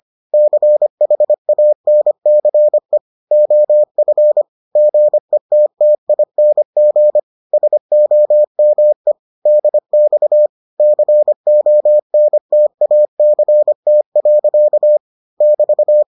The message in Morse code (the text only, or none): CHANCE OF GETTING SOME DX CONTACT. <BT>